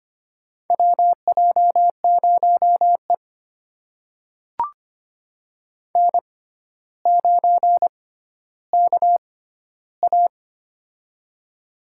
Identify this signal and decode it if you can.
Morse code, heard as a keyed tone, about 25 wpm, spaced as for 13 wpm, 700 Hz. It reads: WJ0E N9KA
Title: USA Call Signs - 4 Characters - 6x Character Spacing - 25wpm